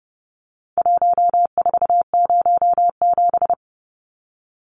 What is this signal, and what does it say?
Morse code, heard as a keyed tone, about 30 wpm, 700 Hz. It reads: 1407
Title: Numbers - Contest Style - 4-Digit 30wpm